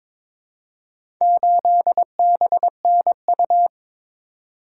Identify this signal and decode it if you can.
Morse code, heard as a keyed tone, about 22 wpm, 700 Hz. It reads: 8BNU
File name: practice / 22wpm / Four Letter-Number Combinations - 22wpm